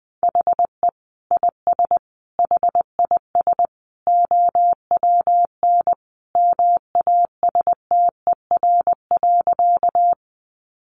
Morse code, heard as a keyed tone, about 19 wpm, 700 Hz. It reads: HE IS HIS OWN MASTER.
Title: Sentences from Top 1300 Words 20wpm